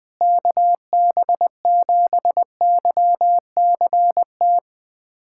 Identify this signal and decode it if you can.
Morse code, heard as a keyed tone, about 21 wpm, 700 Hz. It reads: KB7YCT
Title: USA Callsign DE Callsign 20wpm